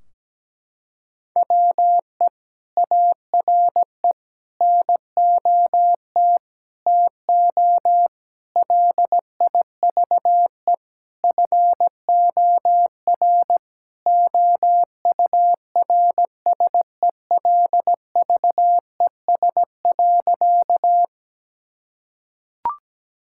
Morse code, heard as a keyed tone, about 17 wpm, 700 Hz. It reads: WE ARE NOT TO LIVE FOR OURSELVES.
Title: Sentences from Top 1600 Words 17wpm